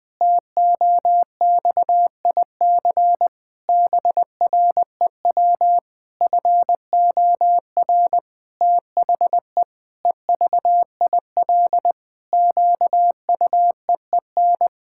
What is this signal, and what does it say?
Morse code, heard as a keyed tone, about 20 wpm, 700 Hz. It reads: TOXIC BREW FOR THE EVIL QUEEN
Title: Pangram Challenge v2 - 20wpm